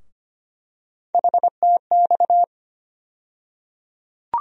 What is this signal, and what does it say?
Morse code, heard as a keyed tone, about 25 wpm, 700 Hz. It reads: HTX
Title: USA Call Sign Postfixes - 3 Letters 25wpm